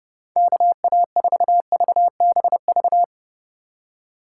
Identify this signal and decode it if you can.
Morse code, heard as a keyed tone, about 30 wpm, 700 Hz. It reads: KA4VBV
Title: USA Call Signs - 6 Characters - 4x Character Spacing - 30wpm